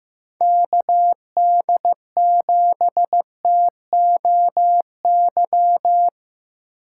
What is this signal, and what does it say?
Morse code, heard as a keyed tone, about 15 wpm, 700 Hz. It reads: KD7TOY
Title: USA Call Signs - 4x Character Spacing - 15wpm